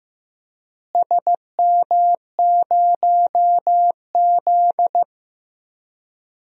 Morse code, heard as a keyed tone, about 16 wpm, 700 Hz. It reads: SM0Z
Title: DXCC-100 Call Signs - 4 Characters - Repeated 3x - 15wpm